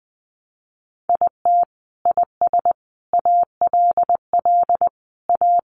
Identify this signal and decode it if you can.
Morse code, heard as a keyed tone, about 20 wpm, 700 Hz. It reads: IT IS ALL A
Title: Sentences from Top 1300 Words 20wpm